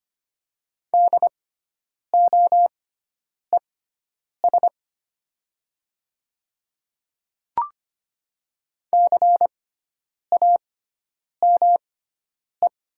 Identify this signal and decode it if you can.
Morse code, heard as a keyed tone, about 25 wpm, spaced as for 9 wpm, 700 Hz. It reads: DOES CAME
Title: 6x Character Spacing - Top 200 Words - 25wpm